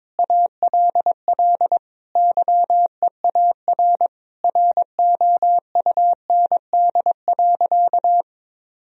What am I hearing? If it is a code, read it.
Morse code, heard as a keyed tone, about 22 wpm, 700 Hz. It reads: ALL YEAR ROUND.
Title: Sentences from Top 1000 Words - Review Part 1 - 22wpm